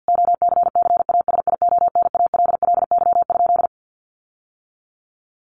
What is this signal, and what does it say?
Morse code, heard as a keyed tone, about 50 wpm, 700 Hz. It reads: OCCASIONALLY?
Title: Sentences from Top 1300 Words 50wpm